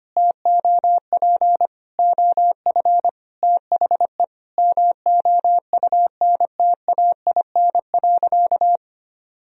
Morse code, heard as a keyed tone, about 25 wpm, 700 Hz. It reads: TOP OF THE MOUNTAIN.
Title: Sentences from Top 1000 Words - Review Part 1 - 25wpm